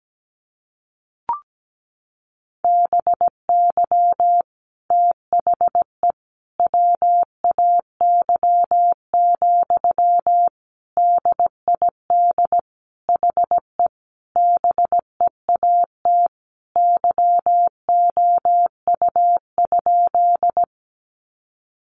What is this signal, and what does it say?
Morse code, heard as a keyed tone, about 17 wpm, 700 Hz. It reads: BY THE WAY, DID HE BEAT YOU?
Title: Sentences from Top 1300 Words 17wpm